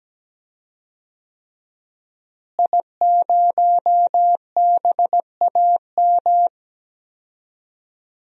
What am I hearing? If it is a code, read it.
Morse code, heard as a keyed tone, about 17 wpm, 700 Hz. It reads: I0BAM
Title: DXCC-100 Call Signs - 5 Characters - Repeated 3x - 17wpm